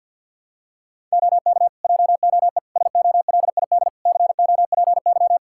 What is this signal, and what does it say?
Morse code, heard as a keyed tone, about 47 wpm, 700 Hz. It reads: OK JOE SOLID COPY
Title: Practice QSO - Exchange 3 - More Info - 50wpm